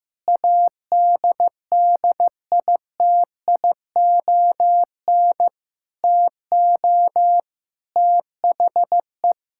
Morse code, heard as a keyed tone, about 14 wpm, 700 Hz. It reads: ADDITION TO THE
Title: Sentences from Top 1000 Words - Review Part 6 - 15wpm